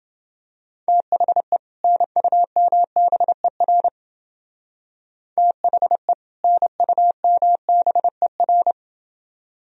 Morse code, heard as a keyed tone, about 31 wpm, 700 Hz. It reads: THE NUMBER THE NUMBER